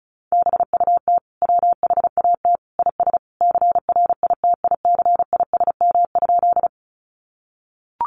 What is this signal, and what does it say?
Morse code, heard as a keyed tone, about 34 wpm, 700 Hz. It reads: BUT WHAT IS CRITICISM?
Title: Sentences from Top 1300 Words 35wpm